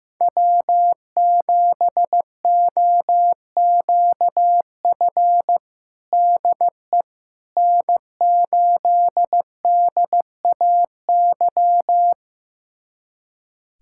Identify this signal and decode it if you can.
Morse code, heard as a keyed tone, about 15 wpm, 700 Hz. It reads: W7OQF DE N8DAY
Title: USA Callsign DE Callsign - 5 Characters - 15wpm